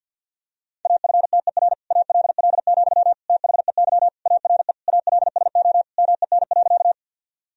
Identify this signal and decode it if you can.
Morse code, heard as a keyed tone, about 51 wpm, 700 Hz. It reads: AFTER ALL, THEY ARE ALSO MEN.
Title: Sentences from Top 1000 Words - Review Part 1 - 50wpm